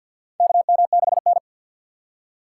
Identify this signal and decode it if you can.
Morse code, heard as a keyed tone, about 52 wpm, 690 Hz. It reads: KM6N
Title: USA Call Signs - 6x Character Spacing - 50wpm